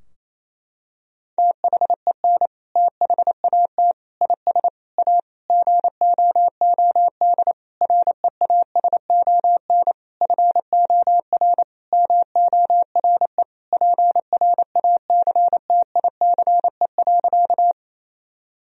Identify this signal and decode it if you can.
Morse code, heard as a keyed tone, about 28 wpm, 700 Hz. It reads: THEN THAT IS A GOOD REASON FOR MORE PRACTICE.
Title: Sentences from Top 1000 Words - Review Part 4 - 28wpm